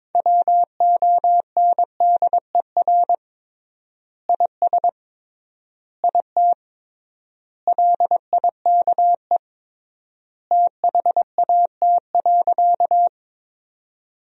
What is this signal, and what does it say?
Morse code, heard as a keyed tone, about 21 wpm, 700 Hz. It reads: WONDER IS IT LIKE THAT.
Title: 3x Word-Spacing - Sentences from Top 500 Words - 22wpm